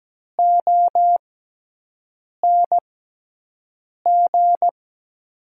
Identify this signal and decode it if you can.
Morse code, heard as a keyed tone, about 17 wpm, spaced as for 6 wpm, 700 Hz. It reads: ONG
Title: 6x Character Spacing - Top 500 Words - 17wpm